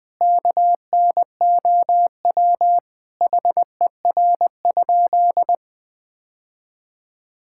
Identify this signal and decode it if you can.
Morse code, heard as a keyed tone, about 21 wpm, 700 Hz. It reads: KNOW HER?
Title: Sentences with 3-Word Phrases - Spoken Beginning - 20wpm